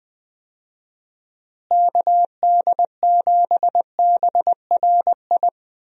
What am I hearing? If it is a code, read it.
Morse code, heard as a keyed tone, about 21 wpm, 700 Hz. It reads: KD7BRI
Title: USA Call Signs - 6 Characters - Repeated 3x - 20wpm